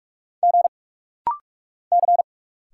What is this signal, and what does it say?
Morse code, heard as a keyed tone, about 46 wpm, 700 Hz. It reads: G C
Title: Single Letters - Rapid-Fire 45wpm